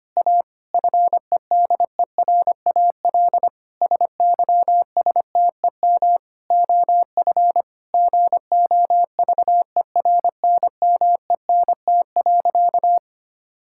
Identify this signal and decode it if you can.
Morse code, heard as a keyed tone, about 25 wpm, 700 Hz. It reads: A FEDERAL SYSTEM OF GOVERNMENT.
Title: Sentences from Top 1000 Words - Review Part 5 - 25wpm